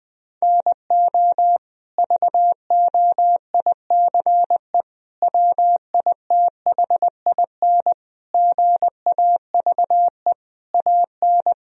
Morse code, heard as a keyed tone, about 20 wpm, 700 Hz. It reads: NO VOICE WITHIN GAVE AN
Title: Sentences from Top 1000 Words - Review Part 4 - 20wpm